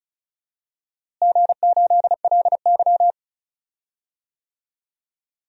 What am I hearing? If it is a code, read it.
Morse code, heard as a keyed tone, about 36 wpm, 690 Hz. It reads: G8LY
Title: DXCC-100 Call Signs - 4 Characters - Repeated 3x - 35wpm